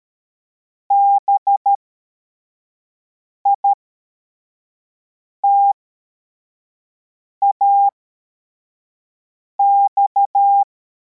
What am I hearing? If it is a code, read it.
Morse code, heard as a keyed tone, about 13 wpm, spaced as for 4 wpm, 800 Hz. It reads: BITAX
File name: formation / cw / Text44